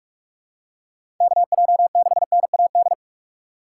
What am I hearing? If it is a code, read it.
Morse code, heard as a keyed tone, about 44 wpm, 700 Hz. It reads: KJ6NAD
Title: USA Call Signs - 6 Characters - 6x Character Spacing - 45wpm